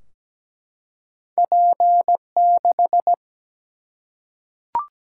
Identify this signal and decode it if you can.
Morse code, heard as a keyed tone, about 17 wpm, 700 Hz. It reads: P6
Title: Single-Letter Single-Number 17wpm